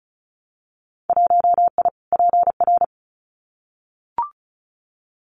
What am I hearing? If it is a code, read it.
Morse code, heard as a keyed tone, about 34 wpm, 700 Hz. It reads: 1I PR
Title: Winter Field Day Exchanges Encore 35wpm